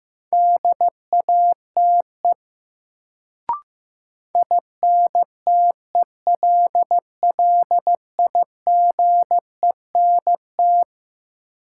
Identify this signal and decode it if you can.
Morse code, heard as a keyed tone, about 15 wpm, 700 Hz. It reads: DATE INTELLIGENT
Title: Sentences from Top 1000 Words - Review Part 7 - New Word Drill - 15wpm